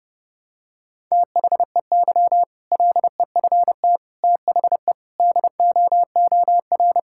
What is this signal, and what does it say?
Morse code, heard as a keyed tone, about 30 wpm, 700 Hz. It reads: THEY LEFT THE DOOR
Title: Sentences from Top 1000 Words - Review Part 6 - 30wpm